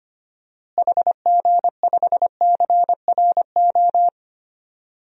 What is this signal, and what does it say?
Morse code, heard as a keyed tone, about 25 wpm, 700 Hz. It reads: HG5CRO
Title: DXCC-100 Call Signs - 6 Characters - Repeated 3x - 25wpm